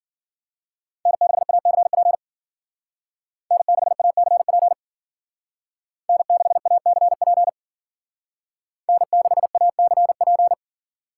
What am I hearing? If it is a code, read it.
Morse code, heard as a keyed tone, about 52 wpm, 700 Hz. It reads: N6ACP N6ACP N6ACP N6ACP